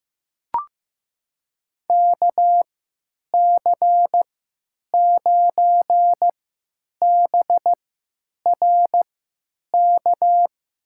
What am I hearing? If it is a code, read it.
Morse code, heard as a keyed tone, about 15 wpm, spaced as for 9 wpm, 700 Hz. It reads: KC9BRK
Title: USA Call Signs - 6 Characters - 3x Character Spacing - 15wpm